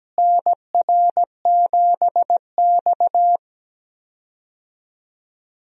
Morse code, heard as a keyed tone, about 17 wpm, 700 Hz. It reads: NR7X
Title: USA Call Signs - 4 Characters - Repeated 3x - 17wpm